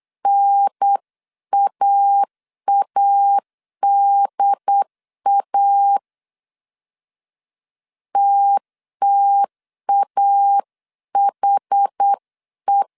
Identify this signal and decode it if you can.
Morse code, heard as a keyed tone, about 8 wpm, 800 Hz. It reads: NAADA TTAHE